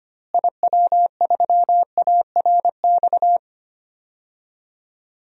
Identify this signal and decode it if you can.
Morse code, heard as a keyed tone, about 25 wpm, 700 Hz. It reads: IW3ARX
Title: DXCC-100 Call Signs - 6 Characters - Repeated 3x - 25wpm